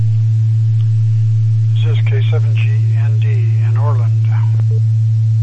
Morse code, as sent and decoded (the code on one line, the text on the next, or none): none